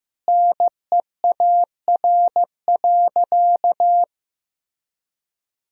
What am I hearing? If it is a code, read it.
Morse code, heard as a keyed tone, about 15 wpm, 700 Hz. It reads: NEAR.
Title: Sentences from Top 1000 Words - Review Part 1 - 15wpm